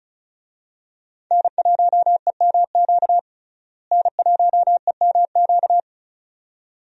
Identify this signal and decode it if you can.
Morse code, heard as a keyed tone, about 36 wpm, 700 Hz. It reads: N1EMQ N1EMQ
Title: USA Call Signs - 5 Characters - Repeated 2x - 35wpm